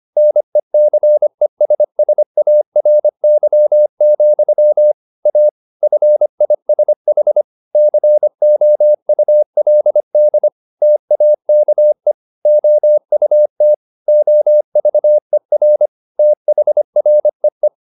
Morse code, heard as a keyed tone, about 25 wpm, 600 Hz. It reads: NECESSARY, A FISH COULD TAKE OUT OVER THREE
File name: the old man and the sea part 1 25wpm